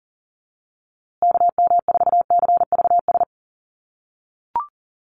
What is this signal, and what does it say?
Morse code, heard as a keyed tone, about 40 wpm, 700 Hz. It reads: KM4CVS
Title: USA Call Signs Repeated 3x 40wpm